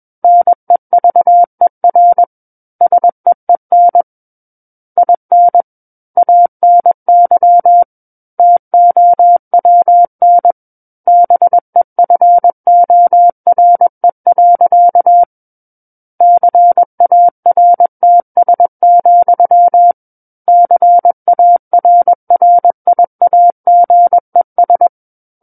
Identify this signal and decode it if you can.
Morse code, heard as a keyed tone, about 21 wpm, 700 Hz. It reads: NEVER SEEN IN ANY TOWN BEFORE. CARTS, CARRIAGES
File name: Book One / 12-Chapter 12 - 21 WPM